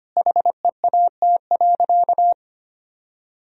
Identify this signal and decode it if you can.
Morse code, heard as a keyed tone, about 25 wpm, 700 Hz. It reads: HEAT.
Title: Sentences from Top 1000 Words - Review Part 3 - 25wpm